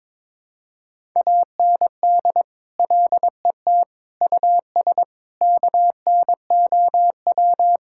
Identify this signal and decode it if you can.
Morse code, heard as a keyed tone, about 22 wpm, 700 Hz. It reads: AND LET US KNOW
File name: Sentences with 7-Word Phrases - 22wpm